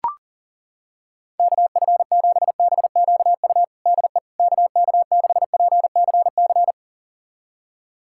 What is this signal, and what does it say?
Morse code, heard as a keyed tone, about 40 wpm, 700 Hz. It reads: KF7BQU DE KK6PCC
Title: USA Callsign DE Callsign 40wpm